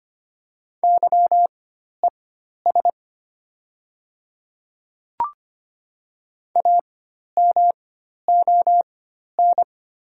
Morse code, heard as a keyed tone, about 25 wpm, spaced as for 12 wpm, 700 Hz. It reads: YES A MON
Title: 4x Character Spacing - Top 500 Words - 25wpm